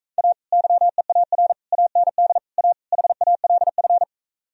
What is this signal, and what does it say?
Morse code, heard as a keyed tone, about 45 wpm, 700 Hz. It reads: A YEAR AND A HALF